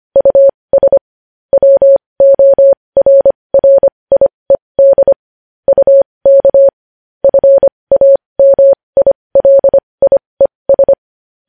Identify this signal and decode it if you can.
Morse code, heard as a keyed tone, about 25 wpm, spaced as for 20 wpm, 550 Hz. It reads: US WORRIED UK FAMILIES
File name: podcast / News headlines-1x-25x20